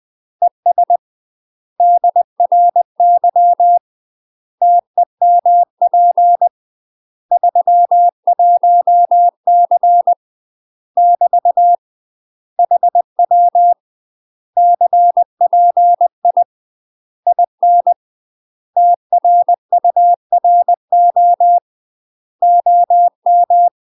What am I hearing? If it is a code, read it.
Morse code, heard as a keyed tone, about 20 wpm, 700 Hz. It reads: ES DRY TEMP 31C <BT> HW CPI IN TRURO OM
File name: UK MASTER QSOs TRAINING / 20WPM / UK MASTER QSOs 0013